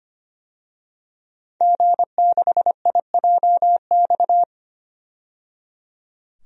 Morse code, heard as a keyed tone, about 25 wpm, 700 Hz. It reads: G6IJX
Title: DXCC-100 Call Signs - 5 Characters - 25wpm